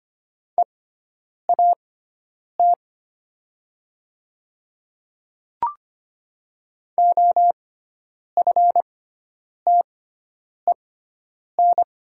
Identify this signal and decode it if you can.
Morse code, heard as a keyed tone, about 25 wpm, spaced as for 9 wpm, 700 Hz. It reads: EAT OFTEN
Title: 6x Character Spacing - Top 300 Words - 25wpm